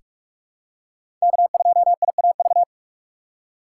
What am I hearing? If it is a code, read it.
Morse code, heard as a keyed tone, about 42 wpm, 700 Hz. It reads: K2IAV
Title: USA Call Signs - 6x Character Spacing - 45wpm